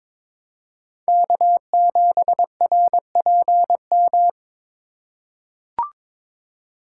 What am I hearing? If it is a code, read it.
Morse code, heard as a keyed tone, about 22 wpm, 690 Hz. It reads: K7RPM
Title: USA Call Signs - 5 Characters - 2x Character Spacing - 22wpm